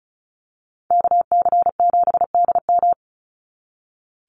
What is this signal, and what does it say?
Morse code, heard as a keyed tone, about 36 wpm, 700 Hz. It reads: KC7DM
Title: NA SOTA Activator Callsigns 35wpm